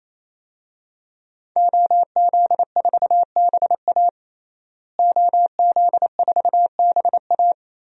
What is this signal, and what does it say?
Morse code, heard as a keyed tone, about 27 wpm, 700 Hz. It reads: OZ4BA OZ4BA
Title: DXCC-100 Call Signs - 5 Characters - Repeated 3x - 28wpm